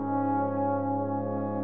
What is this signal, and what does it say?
There is no code.